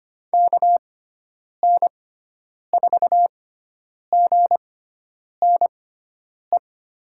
Morse code, heard as a keyed tone, about 25 wpm, spaced as for 9 wpm, 700 Hz. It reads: KN4GNE
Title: USA Call Signs - 6 Characters - 6x Character Spacing - 25wpm